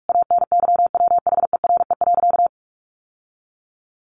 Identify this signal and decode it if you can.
Morse code, heard as a keyed tone, about 44 wpm, 700 Hz. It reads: ANYWHERE.
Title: Sentences from Top 1400 Words 45wpm